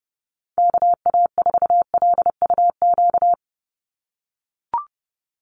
Morse code, heard as a keyed tone, about 30 wpm, 700 Hz. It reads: KA4LUQ
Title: USA Call Signs 30wpm